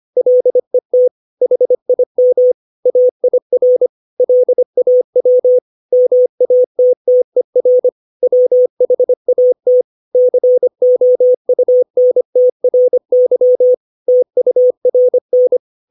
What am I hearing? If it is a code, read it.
Morse code, heard as a keyed tone, about 25 wpm, 500 Hz. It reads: LET HIM AIR LAW MATTER WHAT COUNTRY TURN
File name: 632 words / 632 Words 25WPM